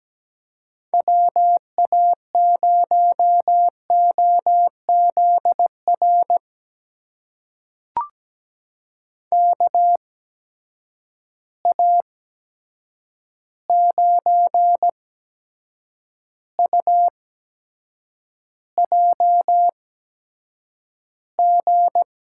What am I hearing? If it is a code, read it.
Morse code, heard as a keyed tone, about 17 wpm, spaced as for 7 wpm, 700 Hz. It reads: WA0OZR KA9UJG
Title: USA Call Signs - 6 Characters - 8x Character Spacing - 17wpm